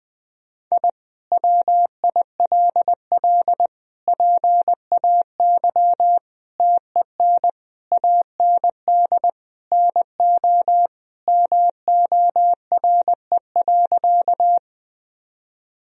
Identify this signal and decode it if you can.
Morse code, heard as a keyed tone, about 21 wpm, 700 Hz. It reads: I WILL PAY TEN AND NO MORE.
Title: Sentences from Top 1000 Words - Review Part 2 - 20wpm